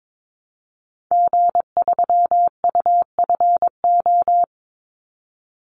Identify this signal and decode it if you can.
Morse code, heard as a keyed tone, about 22 wpm, 700 Hz. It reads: G3UFO
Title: Call Signs Repeated 3x - 22wpm